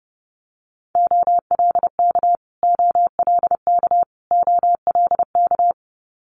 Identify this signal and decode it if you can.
Morse code, heard as a keyed tone, about 30 wpm, 700 Hz. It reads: OLK OLK OLK
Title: Multi-letter Phonemes Repeated 3x 30wpm